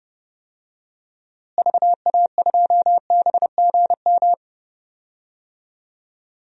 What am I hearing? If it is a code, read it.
Morse code, heard as a keyed tone, about 30 wpm, 700 Hz. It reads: VA2BGM
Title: DXCC-100 Call Signs - 6 Characters - Repeated 2x - 30wpm